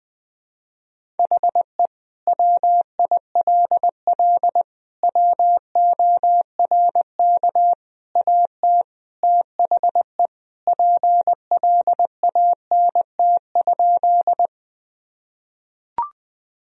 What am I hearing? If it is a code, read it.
Morse code, heard as a keyed tone, about 19 wpm, 700 Hz. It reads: HE WILL WORK AT THE PLANT?
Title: Sentences from Top 1000 Words - Review Part 1 - 20wpm